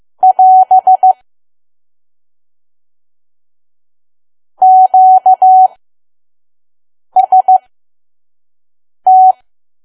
Morse code, heard as a keyed tone, about 15 wpm, spaced as for 5 wpm, 750 Hz. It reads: <AS> QST
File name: Archive / 5 WPM files / PM